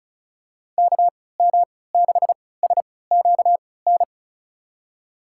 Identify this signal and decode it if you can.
Morse code, heard as a keyed tone, about 36 wpm, spaced as for 20 wpm, 700 Hz. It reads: KM6SQN
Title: USA Call Signs - 6 Characters - 3x Character Spacing - 35wpm